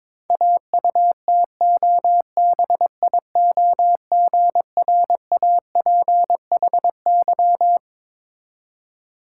Morse code, heard as a keyed tone, about 22 wpm, 690 Hz. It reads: AUTOBIOGRAPHY
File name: Difficult Words with Hints - 22wpm